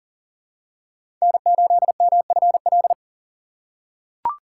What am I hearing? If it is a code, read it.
Morse code, heard as a keyed tone, about 40 wpm, 700 Hz. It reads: N8MFL
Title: USA Call Signs - 5 Characters - 3x Character Spacing - 40wpm